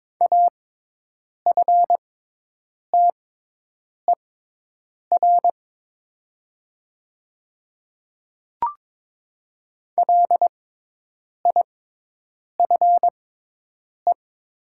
Morse code, heard as a keyed tone, about 22 wpm, spaced as for 8 wpm, 700 Hz. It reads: AFTER LIFE
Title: 6x Character Spacing - Top 500 Words - 22wpm